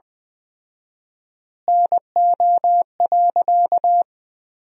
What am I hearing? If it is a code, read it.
Morse code, heard as a keyed tone, about 20 wpm, 700 Hz. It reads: NO.
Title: 4x Word-Spacing - Sentences from Top 500 Words - 20wpm